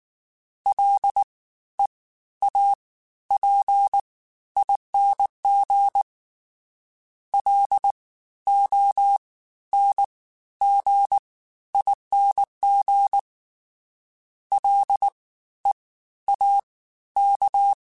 Morse code, heard as a keyed tone, about 19 wpm, spaced as for 12 wpm, 800 Hz. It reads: LEAPING LONGING LEAK
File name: Special INGs-mp3s / Special INGs-mp3s / Fast ing @18